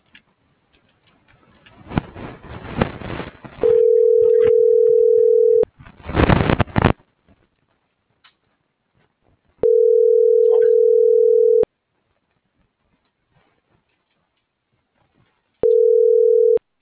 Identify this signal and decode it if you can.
Ringback tone (North America), 440 + 480 Hz.
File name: CENTRAL GENERAL / out yarit.rdz